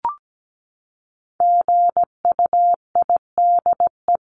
Morse code, heard as a keyed tone, about 17 wpm, 700 Hz. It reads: GUIDE
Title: Top 501-750 Words 17wpm